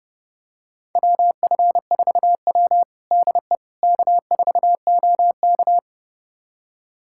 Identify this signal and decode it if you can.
Morse code, heard as a keyed tone, about 30 wpm, 700 Hz. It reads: WF4W DE K4OK
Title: USA Callsign DE Callsign - 4 Characters - 30wpm